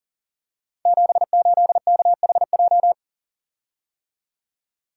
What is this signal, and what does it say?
Morse code, heard as a keyed tone, about 42 wpm, 700 Hz. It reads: 78KHJ
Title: Code Groups - 5 Letter:Numbers - 40wpm